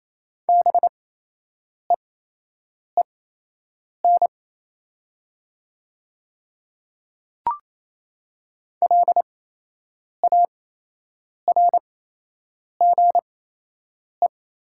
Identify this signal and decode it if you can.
Morse code, heard as a keyed tone, about 28 wpm, spaced as for 8 wpm, 700 Hz. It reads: BEEN LARGE
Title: 8x Character Spacing - Top 200 Words - 28wpm